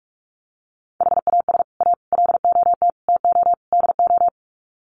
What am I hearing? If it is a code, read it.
Morse code, heard as a keyed tone, about 44 wpm, 700 Hz. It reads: HAS A LOT TO DO
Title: Sets of 5 Words Encore2 45wpm